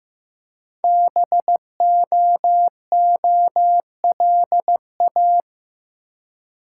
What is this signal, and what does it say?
Morse code, heard as a keyed tone, about 15 wpm, 700 Hz. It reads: BOOLA
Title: Code Groups - 5 Letters - 15wpm